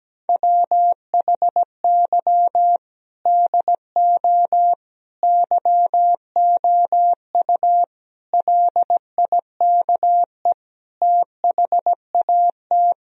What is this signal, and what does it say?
Morse code, heard as a keyed tone, about 17 wpm, 700 Hz. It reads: WHY DO YOU LIKE THAT
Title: Sentences from Top 1000 Words - Review Part 4 - 17wpm